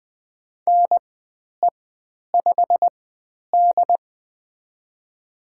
Morse code, heard as a keyed tone, about 20 wpm, spaced as for 10 wpm, 700 Hz. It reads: NE5D
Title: 095 Lesson 16g - Letter c - Call Signs 20wpm